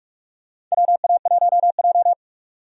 Morse code, heard as a keyed tone, about 46 wpm, 700 Hz. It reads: WA1J
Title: USA Call Signs - 4 Characters - 45wpm